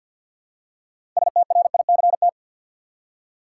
Morse code, heard as a keyed tone, about 49 wpm, 700 Hz. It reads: STRICT
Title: Six Letter Words - 50wpm